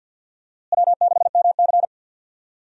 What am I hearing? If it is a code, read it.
Morse code, heard as a keyed tone, about 51 wpm, 700 Hz. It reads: W6MC
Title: USA Call Signs - 4 Characters - Repeated 2x - 50wpm